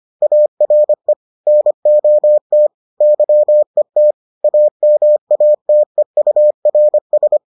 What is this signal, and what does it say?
Morse code, heard as a keyed tone, about 25 wpm, 600 Hz. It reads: ARE NOT YET AMATEURS